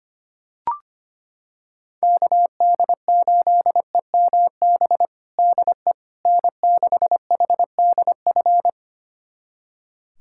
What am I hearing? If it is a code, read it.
Morse code, heard as a keyed tone, about 25 wpm, 700 Hz. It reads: KD8EMB DE N6HDF
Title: USA Callsign DE Callsign 25wpm